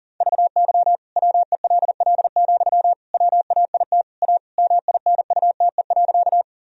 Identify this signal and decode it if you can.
Morse code, heard as a keyed tone, about 37 wpm, 700 Hz. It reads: VY WELL, WAIT A MINUTE.
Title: Sentences from Top 500 Words and Abbreviations 40wpm